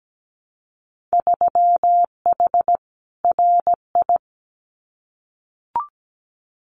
Morse code, heard as a keyed tone, about 17 wpm, 700 Hz. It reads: 3H RI
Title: Winter Field Day Exchanges Encore 17wpm